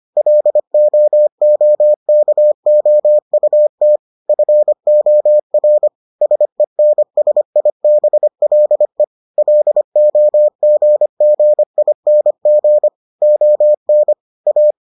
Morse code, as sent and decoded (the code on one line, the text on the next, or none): .-.. --- --- -.- --- ..- - / ..-. --- .-. / ... -. ... .. -... .-.. / .-.. --- --. --. .. -. --. / --- -. / .-
LOOKOUT FOR SENSIBLE LOGGING ON A